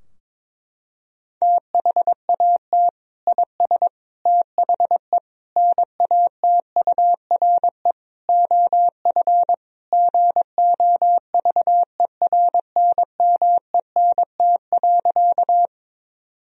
Morse code, heard as a keyed tone, about 22 wpm, 700 Hz. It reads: THAT IS THE NATURE OF GOVERNMENT.
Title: Sentences from Top 1000 Words - Review Part 3 - 22wpm